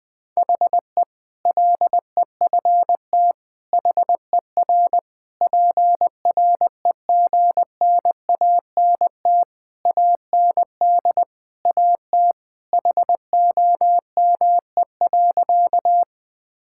Morse code, heard as a keyed tone, about 19 wpm, 700 Hz. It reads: HE LEFT HER PREGNANT AND AT HOME.
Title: Sentences from Top 1000 Words - Review Part 6 - 20wpm